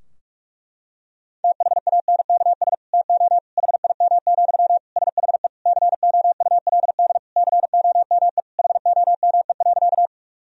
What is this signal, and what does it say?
Morse code, heard as a keyed tone, about 46 wpm, 700 Hz. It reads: THANKS TO HIM, SHE COULD COME HOME.
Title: Sentences from Top 1000 Words - Review Part 3 - 45wpm